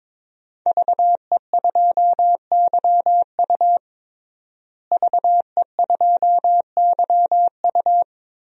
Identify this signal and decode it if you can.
Morse code, heard as a keyed tone, about 22 wpm, 700 Hz. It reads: VE2YU VE2YU
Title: DXCC-100 Call Signs - 5 Characters - Repeated 2x - 22wpm